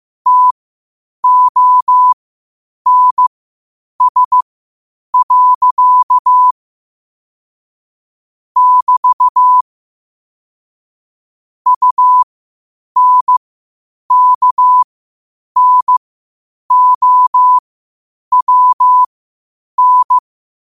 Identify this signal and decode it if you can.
Morse code, heard as a keyed tone, about 15 wpm, spaced as for 8 wpm, 1000 Hz. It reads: TONS. <BT> UNKNOWN
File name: qotdfiles / 8wpm